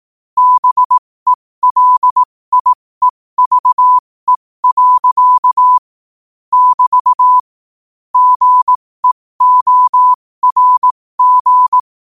Morse code, heard as a keyed tone, about 18 wpm, spaced as for 15 wpm, 1000 Hz. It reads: BELIEVE. <BT> GEORG
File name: qotdfiles / 15wpm